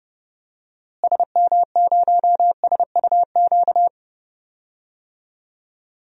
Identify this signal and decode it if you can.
Morse code, heard as a keyed tone, about 30 wpm, 700 Hz. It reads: SM0SUQ
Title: DXCC-100 Call Signs - 6 Characters - Repeated 3x - 30wpm